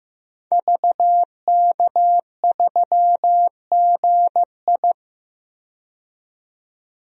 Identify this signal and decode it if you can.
Morse code, heard as a keyed tone, about 15 wpm, 700 Hz. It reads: VK3GI
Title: DXCC-100 Call Signs - 5 Characters - 15wpm